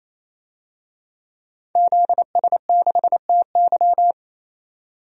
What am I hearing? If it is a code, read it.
Morse code, heard as a keyed tone, about 28 wpm, 700 Hz. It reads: ZS6TY
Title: DXCC-100 Call Signs - 5 Characters - 28wpm